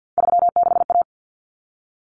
Morse code, heard as a keyed tone, about 50 wpm, 700 Hz. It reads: 36A
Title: Common Cut Numbers - 3-Digit 50wpm